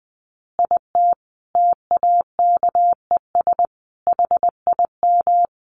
Morse code, heard as a keyed tone, about 19 wpm, 700 Hz. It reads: IT TAKES HIM
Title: Additional Easy Phrases 20wpm